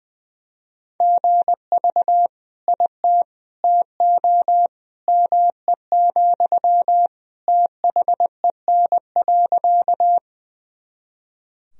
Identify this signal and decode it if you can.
Morse code, heard as a keyed tone, about 20 wpm, 700 Hz. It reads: GV IT TO ME, THEN.
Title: Sentences from Top 200 Words and Abbreviations 20wpm